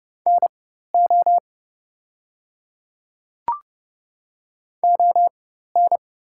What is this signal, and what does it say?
Morse code, heard as a keyed tone, about 29 wpm, spaced as for 14 wpm, 700 Hz. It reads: NO ON
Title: 4x Character Spacing - Top 100 Words - 30wpm